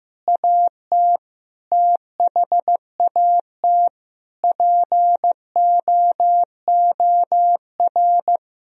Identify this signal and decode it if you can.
Morse code, heard as a keyed tone, about 15 wpm, 700 Hz. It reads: AT THAT POOR
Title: Sentences from Top 1000 Words - Review Part 2 - 15wpm